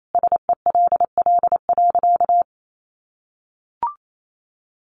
Morse code, heard as a keyed tone, about 28 wpm, 700 Hz. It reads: SELL.
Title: Sentences from Top 1300 Words 28wpm